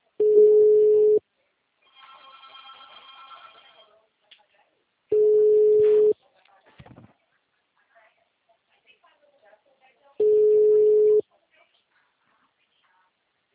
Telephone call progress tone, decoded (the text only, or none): ringback tone